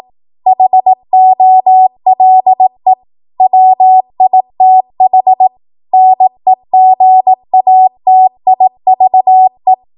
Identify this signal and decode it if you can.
Morse code, heard as a keyed tone, about 18 wpm, 750 Hz. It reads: HOLE WITH NEGATIVE
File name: Morse / 18WPM